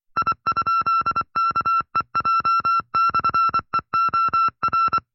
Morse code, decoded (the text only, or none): I?KEJ/EOR